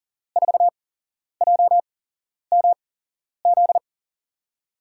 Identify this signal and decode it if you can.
Morse code, heard as a keyed tone, about 40 wpm, spaced as for 11 wpm, 700 Hz. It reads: 4JMZ
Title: USA Call Signs - 6 Characters - 8x Character Spacing - 40wpm